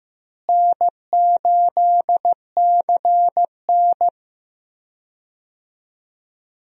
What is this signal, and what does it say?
Morse code, heard as a keyed tone, about 15 wpm, 700 Hz. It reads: N8CN
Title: USA Call Signs - 4 Characters - Repeated 2x - 15wpm